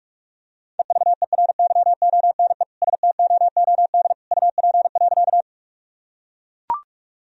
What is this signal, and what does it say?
Morse code, heard as a keyed tone, about 42 wpm, 700 Hz. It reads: EVERYONE STOOD UP.
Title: Sentences from Top 1000 Words - Review Part 2 - 45wpm